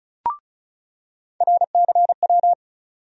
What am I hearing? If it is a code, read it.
Morse code, heard as a keyed tone, about 36 wpm, 700 Hz. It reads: RCW
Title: USA Call Sign Postfixes - 3 Letters 35wpm